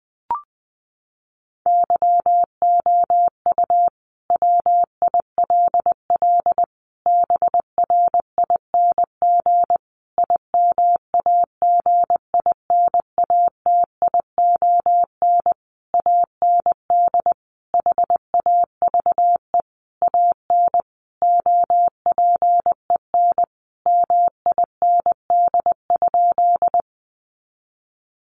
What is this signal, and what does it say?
Morse code, heard as a keyed tone, about 20 wpm, 700 Hz. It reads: YOU WILL BRING IMAGINATION AND HAVE AN OPEN MIND?
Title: Sentences from Top 1300 Words 20wpm